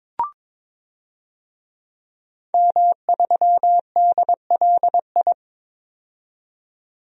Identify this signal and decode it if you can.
Morse code, heard as a keyed tone, about 21 wpm, 700 Hz. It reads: M3DLI
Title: DXCC-100 Call Signs - 5 Characters - 22wpm